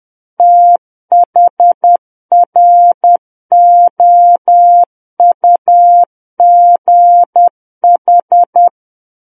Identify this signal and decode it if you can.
Morse code, heard as a keyed tone, about 10 wpm, 700 Hz. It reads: THROUGH
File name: Book One / 01-Chapter 1 - 10 WPM